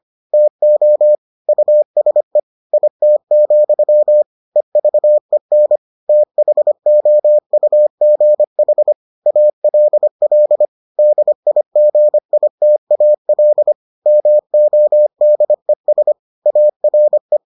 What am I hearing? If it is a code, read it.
Morse code, heard as a keyed tone, about 25 wpm, 600 Hz. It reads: TO USE IT, EVEN THOUGH ALL DIGITAL MODES ARE